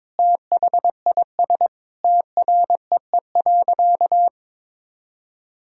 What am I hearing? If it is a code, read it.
Morse code, heard as a keyed tone, about 22 wpm, 700 Hz. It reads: THIS TREE.
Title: Sentences from Top 1000 Words - Review Part 1 - 22wpm